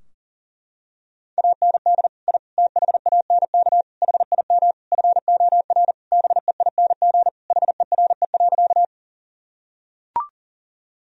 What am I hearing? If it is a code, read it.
Morse code, heard as a keyed tone, about 40 wpm, 700 Hz. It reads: AND I THANK HIM FOR BEING HERE.
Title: Sentences from Top 1000 Words - Review Part 3 - 40wpm